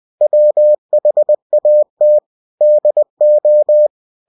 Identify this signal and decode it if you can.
Morse code, heard as a keyed tone, about 20 wpm, 600 Hz. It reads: WHAT DO
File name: cw-books / winnie-the-pooh / winnie-20wpm-page-0087